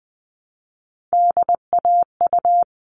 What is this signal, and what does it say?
Morse code, heard as a keyed tone, about 20 wpm, 700 Hz. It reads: DAU
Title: All Cut Numbers - 3-Digit 20wpm